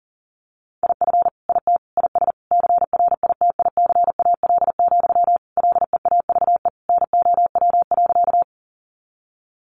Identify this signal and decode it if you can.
Morse code, heard as a keyed tone, about 40 wpm, 700 Hz. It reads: IF IT IS CRITICAL, LEAVE NOW.
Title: Sentences from Top 1100 Words 40wpm